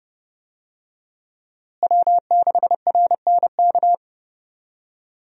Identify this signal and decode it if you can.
Morse code, heard as a keyed tone, about 32 wpm, 700 Hz. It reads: W6RNK
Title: USA Call Signs - 5 Characters - Repeated 2x - 30wpm